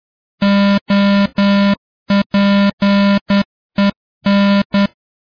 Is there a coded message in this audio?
Yes: Morse code from 0.4 to 4.9 s, heard as a keyed tone, about 10 wpm, 200 Hz. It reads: OPEN